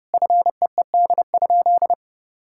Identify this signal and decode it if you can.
Morse code, heard as a keyed tone, about 30 wpm, 700 Hz. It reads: FEED?
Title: Sentences from Top 1600 Words 30wpm